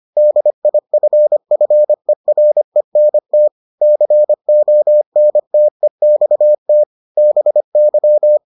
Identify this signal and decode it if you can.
Morse code, heard as a keyed tone, about 25 wpm, 600 Hz. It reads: DIFFERENT CONTEXT BY